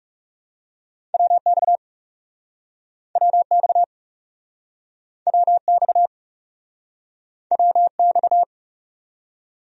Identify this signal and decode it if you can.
Morse code, heard as a keyed tone, about 40 wpm, 700 Hz. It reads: WX WX WX WX